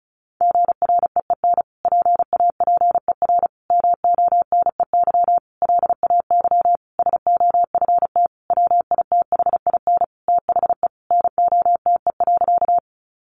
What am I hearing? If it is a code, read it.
Morse code, heard as a keyed tone, about 34 wpm, 700 Hz. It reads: GREEN PAPER MONEY LAY SOFT WITHIN THE NOTE.
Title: Sentences from Top 1500 Words 35wpm